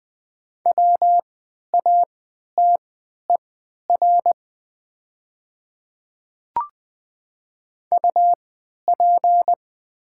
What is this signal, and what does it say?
Morse code, heard as a keyed tone, about 20 wpm, spaced as for 11 wpm, 700 Hz. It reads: WATER UP
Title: 3x Character Spacing - Top 100 Words - 20wpm